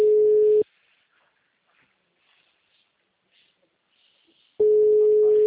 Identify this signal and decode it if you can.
Ringback tone (Europe, 425 Hz).
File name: out- -2019- - - .1195